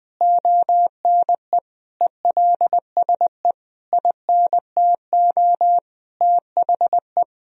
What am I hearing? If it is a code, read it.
Morse code, heard as a keyed tone, about 20 wpm, 700 Hz. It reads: ONE ELSE INTO THE
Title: Sentences from Top 1000 Words - Review Part 2 - 20wpm